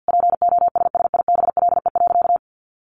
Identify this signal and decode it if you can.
Morse code, heard as a keyed tone, about 50 wpm, 700 Hz. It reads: POSSIBLE.
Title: Sentences from Top 1100 Words 50wpm